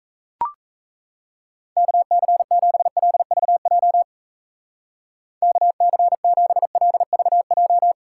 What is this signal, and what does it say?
Morse code, heard as a keyed tone, about 35 wpm, 700 Hz. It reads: KC7LVJ KC7LVJ
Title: Speed Racing - USA 2x3 Call Signs 28wpm